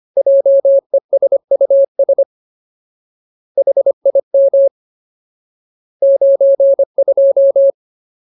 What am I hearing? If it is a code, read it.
Morse code, heard as a keyed tone, about 25 wpm, 550 Hz. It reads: JESUS HIM 92
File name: Ultimate Word List / 25WPM Ultimate